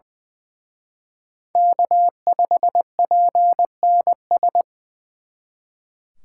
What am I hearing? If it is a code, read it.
Morse code, heard as a keyed tone, about 24 wpm, 700 Hz. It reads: K5PNS
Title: Speed Racing - USA 1x3 Call Signs 20wpm